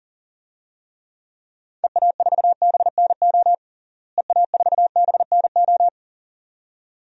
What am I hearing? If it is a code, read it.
Morse code, heard as a keyed tone, about 40 wpm, 700 Hz. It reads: EA4BNO EA4BNO
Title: DXCC-100 Call Signs - 6 Characters - Repeated 2x - 40wpm